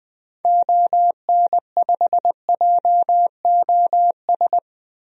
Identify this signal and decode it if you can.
Morse code, heard as a keyed tone, about 20 wpm, 700 Hz. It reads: ON5JOS
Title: DXCC-100 Call Signs - 6 Characters - Repeated 2x - 20wpm